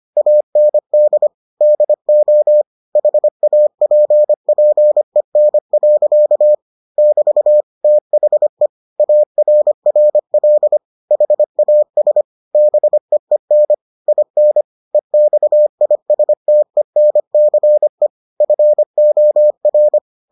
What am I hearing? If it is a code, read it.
Morse code, heard as a keyed tone, about 25 wpm, 600 Hz. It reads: AND DO HAPPEN. <BT> THE ARRL HAS BEEN IN EXISTENCE FOR